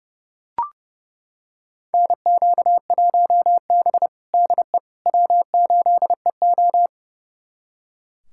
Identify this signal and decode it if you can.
Morse code, heard as a keyed tone, about 30 wpm, 700 Hz. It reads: NQ1B DE W8EO
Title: USA Callsign DE Callsign - 4 Characters - 30wpm